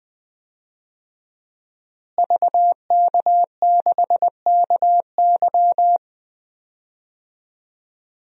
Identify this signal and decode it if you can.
Morse code, heard as a keyed tone, about 20 wpm, 700 Hz. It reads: VK6KY
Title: DXCC-100 Call Signs - 5 Characters - Repeated 2x - 20wpm